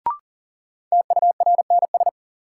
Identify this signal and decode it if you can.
Morse code, heard as a keyed tone, about 40 wpm, 700 Hz. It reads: TURNS
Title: Sets of 5 Words Encore - New Word Drill - 40wpm